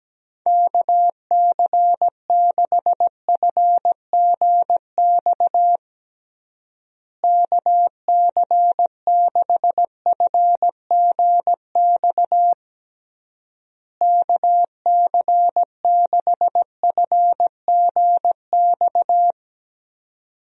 Morse code, heard as a keyed tone, about 17 wpm, 700 Hz. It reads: KC6FGX KC6FGX KC6FGX
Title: USA Call Signs - 6 Characters - Repeated 3x - 17wpm